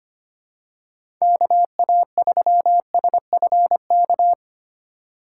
Morse code, heard as a keyed tone, about 25 wpm, 700 Hz. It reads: KA3SFK
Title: USA Call Signs - 6 Characters - 2x Character Spacing - 25wpm